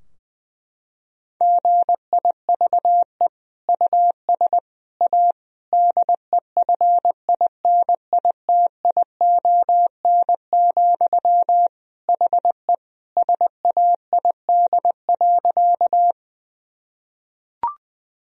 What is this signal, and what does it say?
Morse code, heard as a keyed tone, about 21 wpm, 700 Hz. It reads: GIVE US A DEFINITION, HE SAID.
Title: Sentences from Top 1000 Words - Review Part 4 - 20wpm